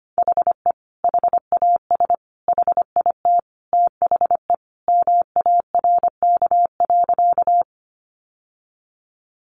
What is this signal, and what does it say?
Morse code, heard as a keyed tone, about 25 wpm, 700 Hz. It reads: HE HAS HIT THE MARK.
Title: Sentences from Top 1200 Words 25wpm